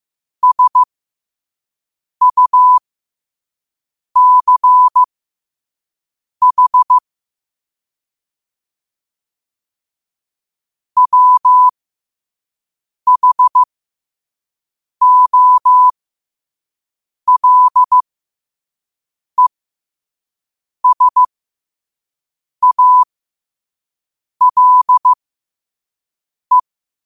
Morse code, heard as a keyed tone, about 15 wpm, spaced as for 5 wpm, 1000 Hz. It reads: SUCH WHOLESALE
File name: qotdfiles / 5wpm